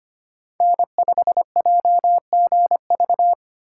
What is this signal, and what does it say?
Morse code, heard as a keyed tone, about 25 wpm, 700 Hz. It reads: N5JGV
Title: USA Call Signs - 5 Characters - 2x Character Spacing - 25wpm